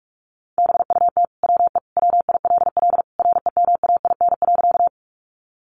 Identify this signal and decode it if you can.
Morse code, heard as a keyed tone, about 44 wpm, 700 Hz. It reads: BUT WE WILL REMAIN.
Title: Sentences from Top 1200 Words 45wpm